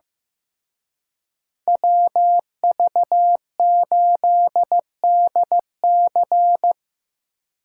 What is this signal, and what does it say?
Morse code, heard as a keyed tone, about 17 wpm, 700 Hz. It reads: WV8DC
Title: Speed Racing - USA 2x2 Call Signs 15wpm